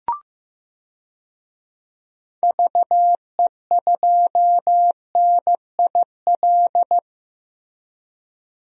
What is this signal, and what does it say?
Morse code, heard as a keyed tone, about 15 wpm, 700 Hz. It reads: VE2NIL
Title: DXCC-100 Call Signs - 6 Characters - 15wpm